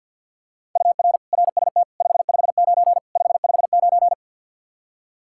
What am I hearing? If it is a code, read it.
Morse code, heard as a keyed tone, about 49 wpm, 700 Hz. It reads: UR RST 559 559
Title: Practice QSO - Exchange 1 - Beginning - 50wpm